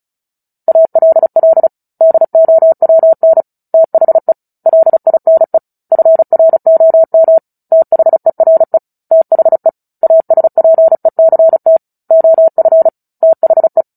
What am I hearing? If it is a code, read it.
Morse code, heard as a keyed tone, about 34 wpm, 650 Hz. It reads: ALL DOWN THE LINE FROM THERE THE ASPECT OF THE
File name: Book Two / 09-Chapter 9 - 35 WPM